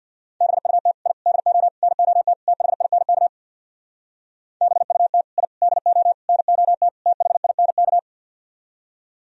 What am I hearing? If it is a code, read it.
Morse code, heard as a keyed tone, about 64 wpm, 700 Hz. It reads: BUT I DO NOT THINK BUT I DO NOT THINK